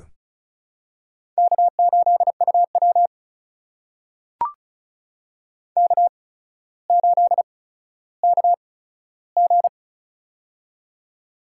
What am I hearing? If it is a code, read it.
Morse code, heard as a keyed tone, about 36 wpm, spaced as for 15 wpm, 700 Hz. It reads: K8UW K8KG
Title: USA Call Signs - 4 Characters - 8x Character Spacing - 35wpm